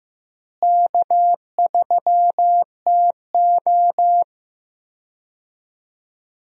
Morse code, heard as a keyed tone, about 15 wpm, 700 Hz. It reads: K3TO
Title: USA Call Signs - 4 Characters - Repeated 3x - 15wpm